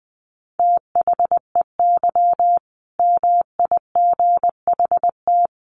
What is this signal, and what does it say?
Morse code, heard as a keyed tone, about 20 wpm, 700 Hz. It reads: THEY MIGHT
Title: A Sound Of Thunder 20wpm